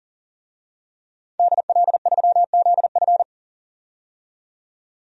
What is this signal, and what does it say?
Morse code, heard as a keyed tone, about 40 wpm, 700 Hz. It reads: DL3ZF
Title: DXCC-100 Call Signs - 5 Characters - 40wpm